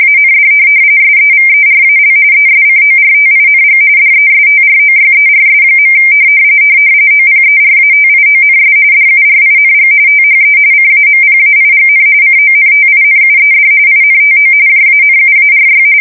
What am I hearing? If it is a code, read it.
RTTY, about 45.45 baud. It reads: AND SEVERAL ESCORTING DESTROYERS. OPERATIONS DURING THE CAMPAIGN CONTINUED INTO JANUARY